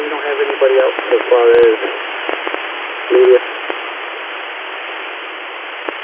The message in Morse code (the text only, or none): none